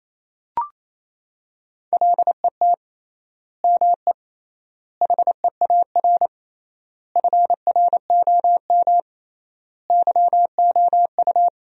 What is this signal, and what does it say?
Morse code, heard as a keyed tone, about 28 wpm, 700 Hz. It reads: LET ME HEAR FROM YOU
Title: 3x Word-Spacing - Sentences from Top 400 Words - 28wpm